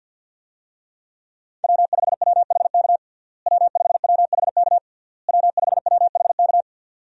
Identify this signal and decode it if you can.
Morse code, heard as a keyed tone, about 52 wpm, 700 Hz. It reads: W5WHK W5WHK W5WHK
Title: USA Call Signs - 5 Characters - Repeated 3x - 50wpm